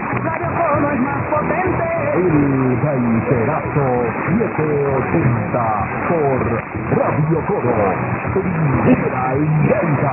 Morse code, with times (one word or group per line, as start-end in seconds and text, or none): none